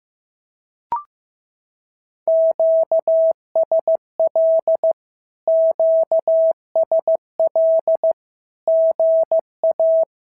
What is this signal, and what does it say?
Morse code, heard as a keyed tone, about 15 wpm, 650 Hz. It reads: QSL QSL GA
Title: SOTA QSO Practice 15wpm